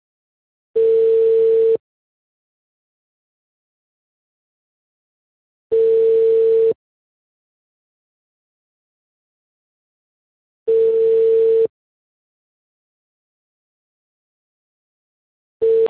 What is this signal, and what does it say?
Ringback tone (China, 450 Hz).